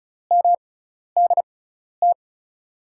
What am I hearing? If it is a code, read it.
Morse code, heard as a keyed tone, about 36 wpm, spaced as for 12 wpm, 700 Hz. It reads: MDT
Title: USA Call Signs - 6x Character Spacing - 35wpm